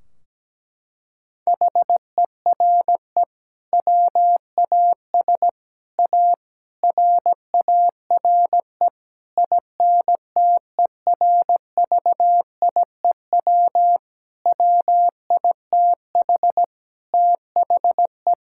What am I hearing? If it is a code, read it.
Morse code, heard as a keyed tone, about 17 wpm, 700 Hz. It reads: HERE WAS A RARE INTERVIEW WITH THE
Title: Sentences from Top 1600 Words 17wpm